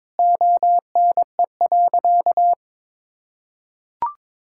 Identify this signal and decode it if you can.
Morse code, heard as a keyed tone, about 22 wpm, 700 Hz. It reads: ONE.
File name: practice / 22wpm / Sentences with 2-Word Phrases - 22wpm